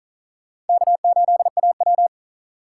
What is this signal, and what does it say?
Morse code, heard as a keyed tone, about 53 wpm, 700 Hz. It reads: K8AW